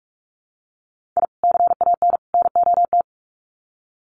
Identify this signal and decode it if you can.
Morse code, heard as a keyed tone, about 44 wpm, 700 Hz. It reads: I CAN NOT
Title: Sets of 3 Words 45wpm